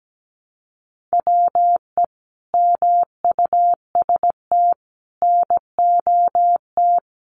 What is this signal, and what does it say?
Morse code, heard as a keyed tone, about 17 wpm, 700 Hz. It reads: WE MUST NOT
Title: Sentences from Top 1100 Words 17wpm